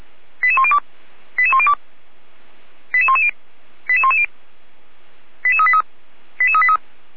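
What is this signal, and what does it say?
Five-tone selective call (ZVEI), heard as quick runs of five tones, tones of about 70 ms. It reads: 88102 88102 88110 88110 88383 88383
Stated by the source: ZVEI five-tone call 88109 received by SDR.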